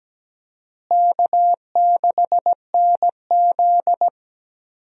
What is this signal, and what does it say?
Morse code, heard as a keyed tone, about 17 wpm, 700 Hz. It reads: K6NZ
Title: USA Call Signs - 4 Characters - Repeated 3x - 17wpm